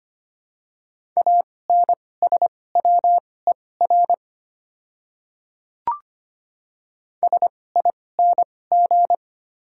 Morse code, heard as a keyed tone, about 25 wpm, spaced as for 18 wpm, 700 Hz. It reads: ANSWER SING
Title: 2x Character Spacing - Top 500 Words - 25wpm